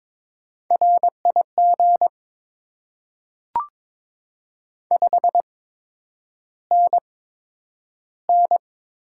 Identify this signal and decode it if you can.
Morse code, heard as a keyed tone, about 22 wpm, spaced as for 9 wpm, 700 Hz. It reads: RIG 5NN
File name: Top 20 QSO Elements - 8x Character Spacing - 22wpm